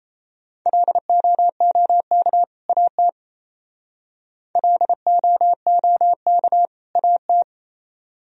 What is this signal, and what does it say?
Morse code, heard as a keyed tone, about 35 wpm, 700 Hz. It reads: LOOK AT LOOK AT